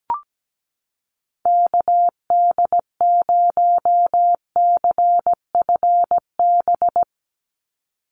Morse code, heard as a keyed tone, about 17 wpm, 700 Hz. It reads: KD0CFB
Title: USA Call Signs 17wpm